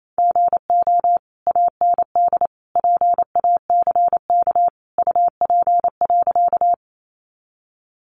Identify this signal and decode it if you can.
Morse code, heard as a keyed tone, about 28 wpm, 700 Hz. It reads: GO AND PACK UP.
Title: Sentences from Top 1300 Words 28wpm